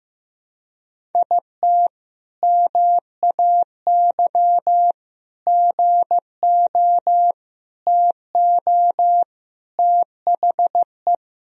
Morse code, heard as a keyed tone, about 15 wpm, 700 Hz. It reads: IT MAY GO TO THE
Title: Sentences from Top 1000 Words - Review Part 1 - 15wpm